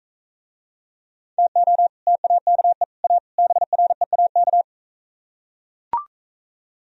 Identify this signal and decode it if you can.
Morse code, heard as a keyed tone, about 43 wpm, 690 Hz. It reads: TO TAKE A BREAK